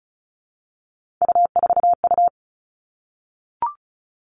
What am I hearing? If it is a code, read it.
Morse code, heard as a keyed tone, about 36 wpm, 700 Hz. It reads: U4U
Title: Three Challenging Letter-Number Combinations 35wpm